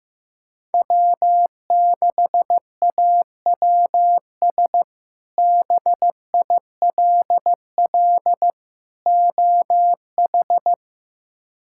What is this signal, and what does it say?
Morse code, heard as a keyed tone, about 15 wpm, 700 Hz. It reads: W6AWS BILL OH
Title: K1USN SST QSO Practice - 15wpm